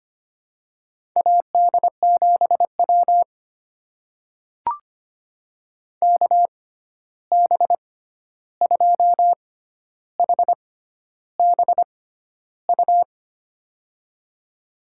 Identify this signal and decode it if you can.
Morse code, heard as a keyed tone, about 25 wpm, spaced as for 11 wpm, 700 Hz. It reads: AD7W KB2HBU
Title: USA Call Signs - 6x Character Spacing - 25wpm